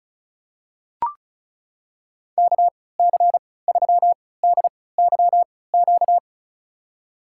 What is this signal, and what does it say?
Morse code, heard as a keyed tone, about 36 wpm, spaced as for 20 wpm, 700 Hz. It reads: KC3DYQ
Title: USA Call Signs - 3x Character Spacing - 35wpm